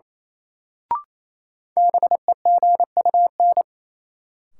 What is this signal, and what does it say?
Morse code, heard as a keyed tone, about 28 wpm, 700 Hz. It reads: BEGUN
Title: Sentences from Top 1000 Words - Review Part 6 - New Word Drill - 28wpm